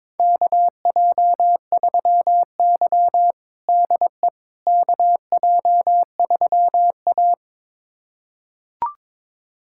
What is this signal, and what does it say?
Morse code, heard as a keyed tone, about 22 wpm, 700 Hz. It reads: KJ3Y DE KJ3A
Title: USA Callsign DE Callsign - 4 Characters - 22wpm